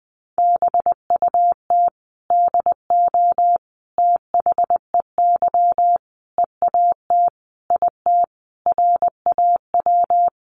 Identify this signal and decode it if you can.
Morse code, heard as a keyed tone, about 20 wpm, 700 Hz. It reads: BUT DO THEY EAT IT RAW
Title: Sentences from Top 1500 Words 20wpm